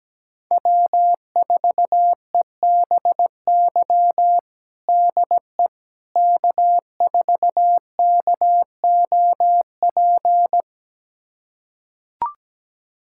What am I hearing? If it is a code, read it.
Morse code, heard as a keyed tone, about 17 wpm, 700 Hz. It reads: W4EBY DE K4KOP
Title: USA Callsign DE Callsign - 5 Characters - 17wpm